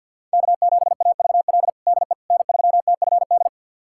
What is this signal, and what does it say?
Morse code, heard as a keyed tone, about 47 wpm, 700 Hz. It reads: K7AVL DE N3TFD
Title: USA Callsign DE Callsign - 5 Characters - 50wpm